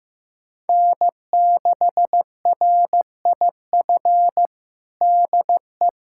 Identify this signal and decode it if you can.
Morse code, heard as a keyed tone, about 15 wpm, 700 Hz. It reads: N6RIF DE
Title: USA Callsign DE Callsign - 5 Characters - 15wpm